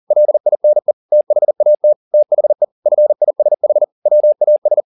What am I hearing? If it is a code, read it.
Morse code, heard as a keyed tone, about 40 wpm, 600 Hz. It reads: LINE THAT THE FISH WAS
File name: the old man and the sea 2 40wpm